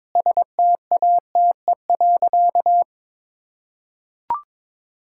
Morse code, heard as a keyed tone, about 22 wpm, 700 Hz. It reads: STATE.
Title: Sentences from Top 1000 Words - Review Part 5 - 22wpm